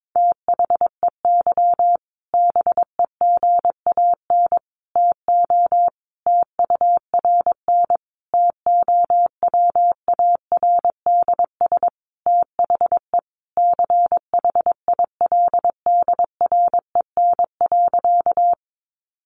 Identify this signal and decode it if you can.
Morse code, heard as a keyed tone, about 22 wpm, 700 Hz. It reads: THEY BEGAN TO TURN TOWARDS THE CHILDREN.
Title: Sentences from Top 1100 Words 22wpm